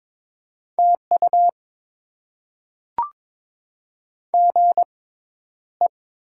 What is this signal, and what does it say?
Morse code, heard as a keyed tone, about 22 wpm, spaced as for 11 wpm, 700 Hz. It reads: TU GE
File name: Top 20 QSO Elements - 6x Character Spacing - 22wpm